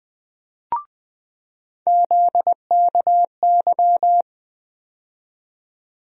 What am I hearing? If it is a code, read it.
Morse code, heard as a keyed tone, about 20 wpm, 700 Hz. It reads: ZKY
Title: USA Call Sign Postfixes - 3 Letters 20wpm